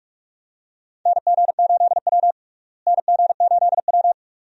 Morse code, heard as a keyed tone, about 43 wpm, 700 Hz. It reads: NG8W NG8W
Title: USA Call Signs - 4 Characters - Repeated 2x - 45wpm